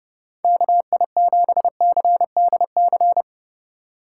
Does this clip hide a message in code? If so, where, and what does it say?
Yes: Morse code from 0.4 to 3.2 s, heard as a keyed tone, about 29 wpm, 700 Hz. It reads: KI7CDC